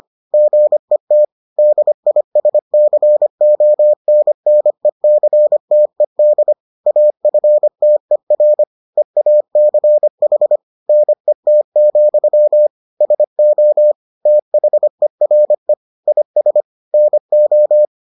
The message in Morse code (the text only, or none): GET DISCONNECTED AFTER EACH NET, SO THERE IS NO